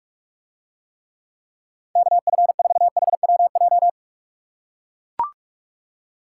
Morse code, heard as a keyed tone, about 44 wpm, 700 Hz. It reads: KF4HWJ